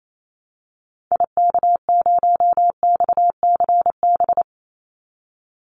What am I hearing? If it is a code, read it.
Morse code, heard as a keyed tone, about 28 wpm, 700 Hz. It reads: IK0XCB
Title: Call Signs - 28wpm